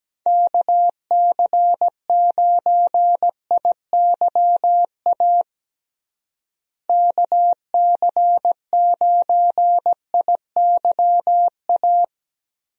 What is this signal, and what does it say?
Morse code, heard as a keyed tone, about 17 wpm, 700 Hz. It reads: KC9IYA KC9IYA
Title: USA Call Signs - 6 Characters - Repeated 3x - 17wpm